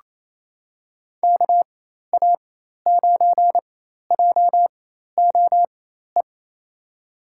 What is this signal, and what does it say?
Morse code, heard as a keyed tone, about 27 wpm, spaced as for 13 wpm, 700 Hz. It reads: KA9JOE
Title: USA Call Signs - 4x Character Spacing - 28wpm